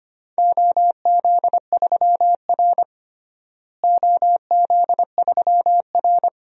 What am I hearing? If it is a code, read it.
Morse code, heard as a keyed tone, about 25 wpm, 700 Hz. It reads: OZ3R OZ3R
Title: DXCC-100 Call Signs - 4 Characters - Repeated 2x - 25wpm